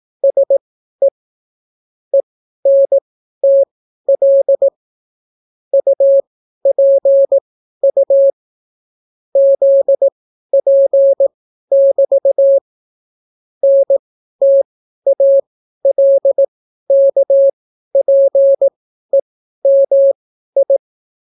Morse code, heard as a keyed tone, about 18 wpm, spaced as for 12 wpm, 550 Hz. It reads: SE ENTL UPU ZP<BT> NTALKPEMI